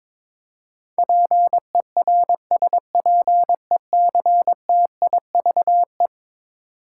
Morse code, heard as a keyed tone, about 22 wpm, 700 Hz. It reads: PERSPECTIVE
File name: practice / 22wpm / Eleven Letter Words - 22wpm